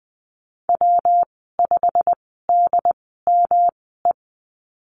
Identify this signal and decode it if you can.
Morse code, heard as a keyed tone, about 20 wpm, spaced as for 14 wpm, 700 Hz. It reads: W5DME
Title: 122 Lesson 20g - Number 3 - Call Signs 20wpm